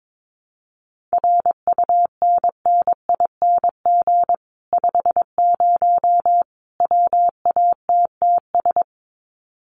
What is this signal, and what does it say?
Morse code, heard as a keyed tone, about 22 wpm, 700 Hz. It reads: RUNNING 50 WATTS
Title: Common QSO Phrases 22wpm